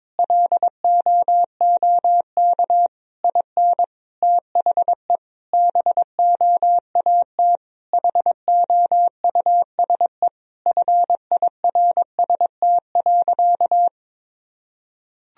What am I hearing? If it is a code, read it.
Morse code, heard as a keyed tone, about 22 wpm, 700 Hz. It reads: LOOK IN THE BOAT HOUSE FIRST.
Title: Sentences from Top 1000 Words - Review Part 3 - 22wpm